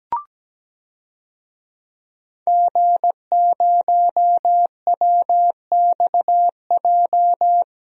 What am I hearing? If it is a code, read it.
Morse code, heard as a keyed tone, about 17 wpm, 700 Hz. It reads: G0WXJ
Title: DXCC-100 Call Signs - 5 Characters - Repeated 2x - 17wpm